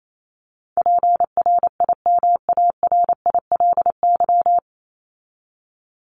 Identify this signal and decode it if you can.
Morse code, heard as a keyed tone, about 28 wpm, 700 Hz. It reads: PRIMARILY
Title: LY Suffix - Top 100 Words with Base Words 28wpm